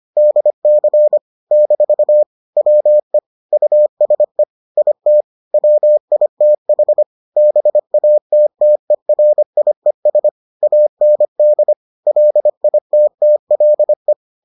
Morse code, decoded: DC - WE USE IT WITH BATTERIES AND LITTLE

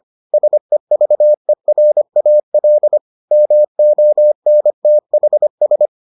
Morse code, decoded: SEVERAL MONTHS

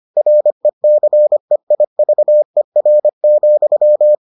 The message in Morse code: RECEIVER,

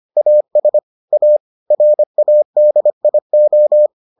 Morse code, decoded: AS A RADIO

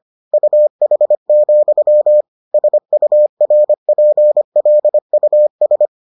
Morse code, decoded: UH, SURPLUS

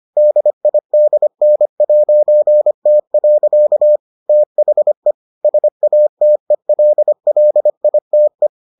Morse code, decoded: DIDN'T. THE SATELLITE

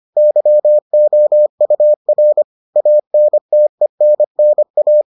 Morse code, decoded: YOUR ANTENNA

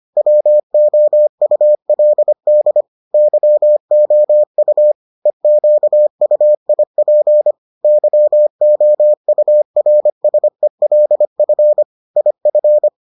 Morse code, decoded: WOULD YOU EQUIP YOURSELF IF

